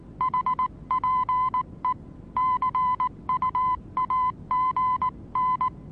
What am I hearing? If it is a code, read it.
Morse code, heard as a keyed tone, about 19 wpm, 1010 Hz. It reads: HPE CUAGN
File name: repos / Morsekurs / DJ4UF Morsen26